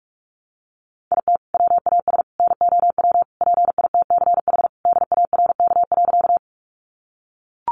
Morse code, heard as a keyed tone, about 44 wpm, 700 Hz. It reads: IT WAS NOW PITCH DARK.
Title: Sentences from Top 1300 Words 45wpm